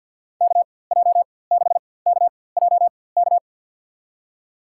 Morse code, heard as a keyed tone, about 50 wpm, spaced as for 23 wpm, 700 Hz. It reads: KJ6KJK
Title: USA Call Signs - 6 Characters - 4x Character Spacing - 50wpm